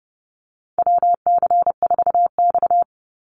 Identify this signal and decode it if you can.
Morse code, heard as a keyed tone, about 30 wpm, 700 Hz. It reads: WC4X
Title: Call Signs Repeated 2x - 30wpm